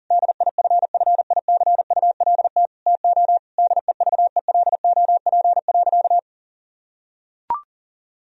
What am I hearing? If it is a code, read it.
Morse code, heard as a keyed tone, about 37 wpm, 700 Hz. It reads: DIFFICULT TO DEVELOP.
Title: Sentences from Top 1000 Words - Review Part 2 - 40wpm